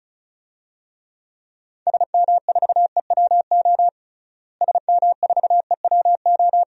Morse code, heard as a keyed tone, about 36 wpm, 700 Hz. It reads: SM4EWO SM4EWO
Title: DXCC-100 Call Signs - 6 Characters - Repeated 3x - 35wpm